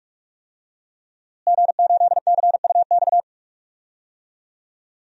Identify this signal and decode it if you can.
Morse code, heard as a keyed tone, about 41 wpm, 700 Hz. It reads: G8CUX
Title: DXCC-100 Call Signs - 5 Characters - Repeated 2x - 45wpm